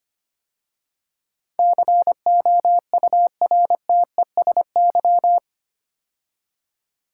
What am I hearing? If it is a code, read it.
Morse code, heard as a keyed tone, about 25 wpm, 700 Hz. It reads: COURTESY
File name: 8 Letter Words - First 3 Letters Given - 25wpm